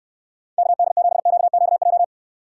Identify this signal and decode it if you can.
Morse code, heard as a keyed tone, about 68 wpm, 700 Hz. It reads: KD7QQJ